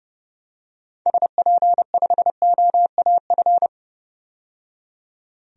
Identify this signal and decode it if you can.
Morse code, heard as a keyed tone, about 31 wpm, 700 Hz. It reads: SP5OAF
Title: DXCC-100 Call Signs - 6 Characters - 30wpm